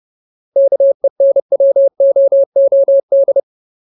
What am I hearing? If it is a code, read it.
Morse code, heard as a keyed tone, about 30 wpm, 550 Hz. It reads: KENWOOD
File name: Ultimate Word List / 30WPM Ultimate